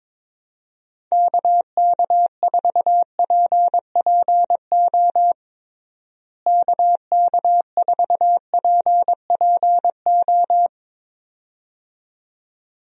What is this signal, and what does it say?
Morse code, heard as a keyed tone, about 22 wpm, 700 Hz. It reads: KK4PPO KK4PPO
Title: USA Call Signs - 6 Characters - Repeated 2x - 22wpm